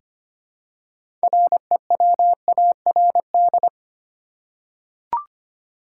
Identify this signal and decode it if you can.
Morse code, heard as a keyed tone, about 25 wpm, 700 Hz. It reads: REWARD
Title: Six Letter Words - 25wpm